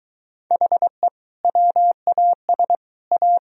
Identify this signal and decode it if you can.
Morse code, heard as a keyed tone, about 20 wpm, 700 Hz. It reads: HE WAS A